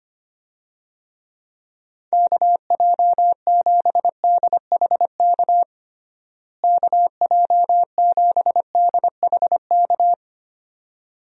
Morse code, heard as a keyed tone, about 25 wpm, 700 Hz. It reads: KJ7DHK KJ7DHK
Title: USA Call Signs - 6 Characters - Repeated 2x - 25wpm